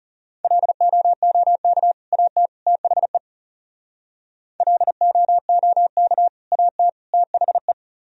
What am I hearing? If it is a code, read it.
Morse code, heard as a keyed tone, about 33 wpm, 700 Hz. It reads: LOOK AT THE LOOK AT THE
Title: Speed-Racing - Sets of 3 Words 30wpm